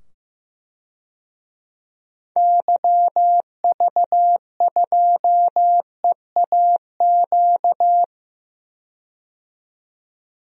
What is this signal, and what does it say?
Morse code, heard as a keyed tone, about 15 wpm, 700 Hz. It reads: YV2EAQ
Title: DXCC-100 Call Signs - 6 Characters - 15wpm